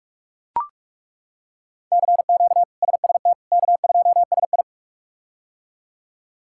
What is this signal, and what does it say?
Morse code, heard as a keyed tone, about 42 wpm, 700 Hz. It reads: CQ SST K2SI
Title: K1USN SST QSO Practice - 45wpm